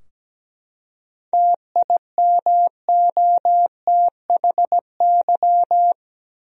Morse code, heard as a keyed tone, about 17 wpm, 700 Hz. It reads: TIMOTHY
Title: USA Common Names - 17wpm